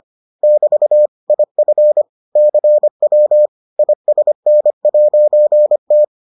Morse code, heard as a keyed tone, about 25 wpm, 600 Hz. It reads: <BT> IF CW ISN'T